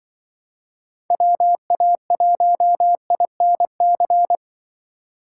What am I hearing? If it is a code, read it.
Morse code, heard as a keyed tone, about 25 wpm, 700 Hz. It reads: WA1INC
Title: Speed Racing - USA 2x3 Call Signs 20wpm